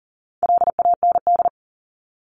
Morse code, heard as a keyed tone, about 38 wpm, 700 Hz. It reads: LAND
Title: Top 500 Words 40wpm